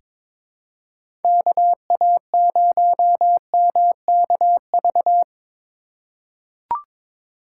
Morse code, heard as a keyed tone, about 22 wpm, 700 Hz. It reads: KA0MKV
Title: USA Call Signs - 6 Characters - 2x Character Spacing - 22wpm